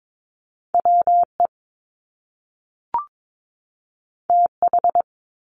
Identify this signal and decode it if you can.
Morse code, heard as a keyed tone, about 22 wpm, 700 Hz. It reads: WE TH
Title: Top Two Letter Combinations in Words 22wpm